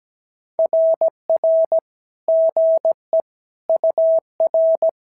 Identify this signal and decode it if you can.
Morse code, heard as a keyed tone, about 17 wpm, 650 Hz. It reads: RR GE UR
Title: SOTA QSO Practice 17wpm